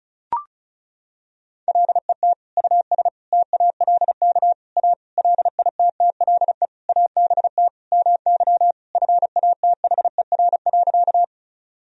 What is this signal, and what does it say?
Morse code, heard as a keyed tone, about 36 wpm, 700 Hz. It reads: LET US TALK A LITTLE ABT MY FATHER.
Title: Sentences from Top 400 Words and Abbreviations 35wpm